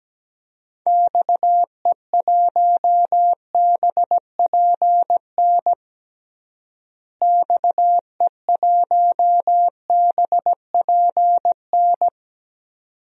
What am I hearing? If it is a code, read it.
Morse code, heard as a keyed tone, about 17 wpm, 700 Hz. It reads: XE1BPN XE1BPN
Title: DXCC-100 Call Signs - 6 Characters - Repeated 2x - 17wpm